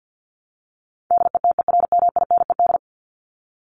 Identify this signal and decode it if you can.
Morse code, heard as a keyed tone, about 50 wpm, 700 Hz. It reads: DETERMINED
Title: ED Suffix - Top 100 Words with Base Words 50wpm